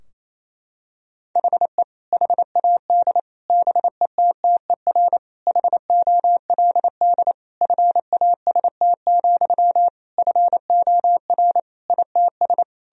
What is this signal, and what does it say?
Morse code, heard as a keyed tone, about 28 wpm, 700 Hz. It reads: HE HAD BETTER HOLD FAST, FOR ITS
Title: Sentences from Top 1000 Words - Review Part 2 - 28wpm